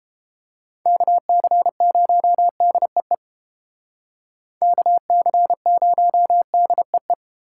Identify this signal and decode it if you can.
Morse code, heard as a keyed tone, about 32 wpm, 700 Hz. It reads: KC0DEE KC0DEE